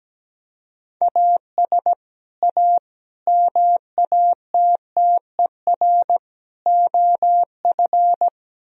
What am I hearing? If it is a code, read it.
Morse code, heard as a keyed tone, about 19 wpm, 700 Hz. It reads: AS A MATTER OF
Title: Speed-Racing - Sets of 4 Words 17wpm